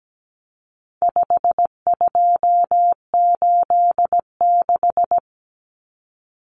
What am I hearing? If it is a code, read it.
Morse code, heard as a keyed tone, about 17 wpm, 700 Hz. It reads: 5286
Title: Numbers - 4-Digit 17wpm